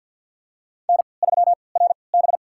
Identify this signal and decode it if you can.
Morse code, heard as a keyed tone, about 49 wpm, spaced as for 28 wpm, 700 Hz. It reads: N3RB
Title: USA Call Signs - 4 Characters - 3x Character Spacing - 50wpm